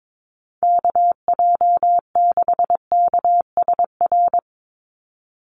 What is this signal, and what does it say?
Morse code, heard as a keyed tone, about 22 wpm, 700 Hz. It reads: KJ6KSR
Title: USA Call Signs 22wpm